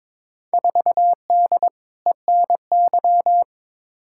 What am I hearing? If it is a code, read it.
Morse code, heard as a keyed tone, about 22 wpm, 700 Hz. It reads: 4D ENY
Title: Field Day Exchanges Encore - 22wpm